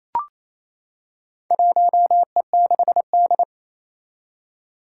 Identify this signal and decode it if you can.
Morse code, heard as a keyed tone, about 28 wpm, 700 Hz. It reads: 1E6D
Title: Four Challenging Letter-Number Combinations 28wpm